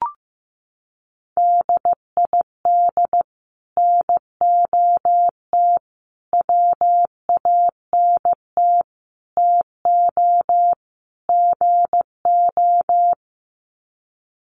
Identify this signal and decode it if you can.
Morse code, heard as a keyed tone, about 15 wpm, 700 Hz. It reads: DID NOT WANT TO GO
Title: Sets of 5 Words 15wpm